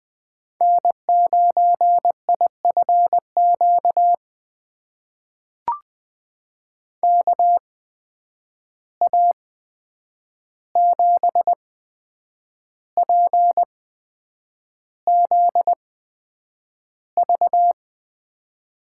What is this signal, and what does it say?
Morse code, heard as a keyed tone, about 20 wpm, spaced as for 8 wpm, 700 Hz. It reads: N9IFQ KA7PZV
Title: USA Call Signs - 8x Character Spacing - 20wpm